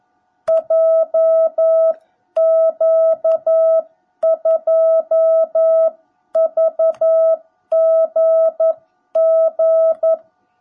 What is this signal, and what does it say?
Morse code, heard as a keyed tone, about 11 wpm, 660 Hz. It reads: JQ2VGG
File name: jh8cbh / 2021cw / cw17b